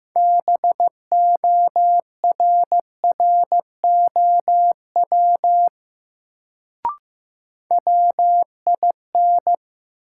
Morse code, heard as a keyed tone, about 16 wpm, 700 Hz. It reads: BORROW WIN
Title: Sentences from Top 1500 Words - New Word Drill - 15wpm